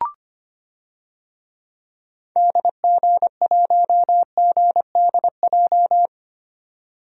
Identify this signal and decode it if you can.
Morse code, heard as a keyed tone, about 25 wpm, 700 Hz. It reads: DG1GDJ
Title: DXCC-100 Call Signs - 6 Characters - 25wpm